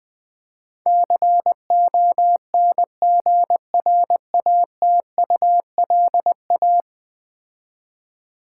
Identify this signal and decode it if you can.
Morse code, heard as a keyed tone, about 20 wpm, 700 Hz. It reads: CONGRATULA
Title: Build A Word - 5 Letters At A Time - 12-Letter Words - 20wpm